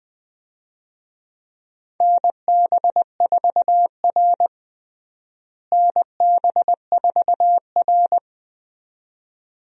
Morse code, heard as a keyed tone, about 21 wpm, 700 Hz. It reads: NB4R NB4R
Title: USA Call Signs - 4 Characters - Repeated 2x - 20wpm